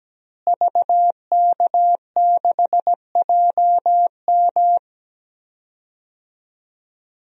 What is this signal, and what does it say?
Morse code, heard as a keyed tone, about 17 wpm, 700 Hz. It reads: VK6JM
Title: DXCC-100 Call Signs - 5 Characters - Repeated 2x - 17wpm